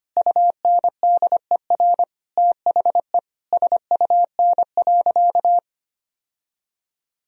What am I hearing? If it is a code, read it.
Morse code, heard as a keyed tone, about 25 wpm, 700 Hz. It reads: UNDER THE SUN.
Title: Sentences from Top 1000 Words - Review Part 1 - 25wpm